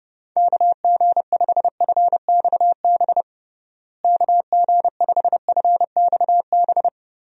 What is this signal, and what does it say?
Morse code, heard as a keyed tone, about 30 wpm, 700 Hz. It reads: KG5FXB KG5FXB
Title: USA Call Signs - 6 Characters - Repeated 3x - 30wpm